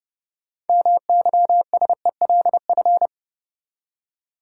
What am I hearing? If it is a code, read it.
Morse code, heard as a keyed tone, about 31 wpm, 700 Hz. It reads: MYSELF
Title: Six Letter Words - 30wpm